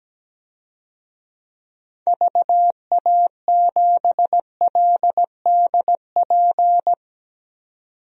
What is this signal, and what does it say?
Morse code, heard as a keyed tone, about 17 wpm, 700 Hz. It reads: VA7LDP
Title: DXCC-100 Call Signs - 6 Characters - 17wpm